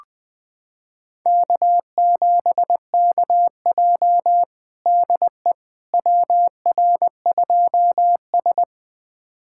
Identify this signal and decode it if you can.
Morse code, heard as a keyed tone, about 20 wpm, 700 Hz. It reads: K7KJ DE WR2S
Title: USA Callsign DE Callsign - 4 Characters - 20wpm